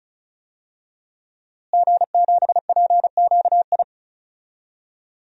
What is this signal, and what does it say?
Morse code, heard as a keyed tone, about 34 wpm, 700 Hz. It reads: G7PQI
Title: DXCC-100 Call Signs - 5 Characters - Repeated 2x - 35wpm